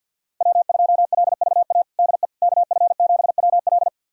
Morse code, heard as a keyed tone, about 49 wpm, 690 Hz. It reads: W2LVA DE KF7WL
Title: USA Callsign DE Callsign - 5 Characters - 50wpm